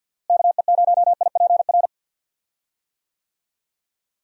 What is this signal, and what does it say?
Morse code, heard as a keyed tone, about 50 wpm, 700 Hz. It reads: KE0IPR
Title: USA Call Signs - 6 Characters - Repeated 3x - 50wpm